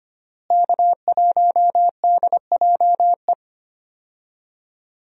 Morse code, heard as a keyed tone, about 25 wpm, 700 Hz. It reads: K1DJE
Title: USA Call Signs - 5 Characters - Repeated 3x - 25wpm